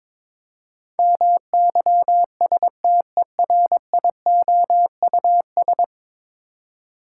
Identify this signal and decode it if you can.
Morse code, heard as a keyed tone, about 23 wpm, 700 Hz. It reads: MYSTERIOUS
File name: practice / 22wpm / 10 Letter Words - First 3 Letters Given - 22wpm